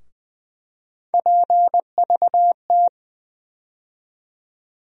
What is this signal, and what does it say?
Morse code, heard as a keyed tone, about 20 wpm, 700 Hz. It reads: PVT
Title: USA Call Sign Postfixes - 3 Letters 20wpm